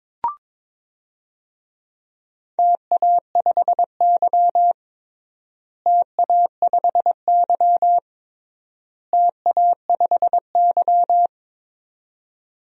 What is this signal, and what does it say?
Morse code, heard as a keyed tone, about 22 wpm, 700 Hz. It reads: TA5Y TA5Y TA5Y
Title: DXCC-100 Call Signs - 4 Characters - Repeated 3x - 22wpm